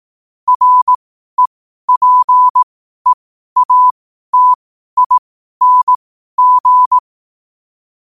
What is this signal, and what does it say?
Morse code, heard as a keyed tone, about 18 wpm, spaced as for 12 wpm, 1000 Hz. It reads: REPEATING
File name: qotdfiles / 12wpm